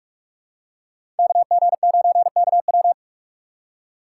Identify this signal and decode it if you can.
Morse code, heard as a keyed tone, about 46 wpm, 700 Hz. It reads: KG9KW
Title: USA Call Signs - 5 Characters - 3x Character Spacing - 45wpm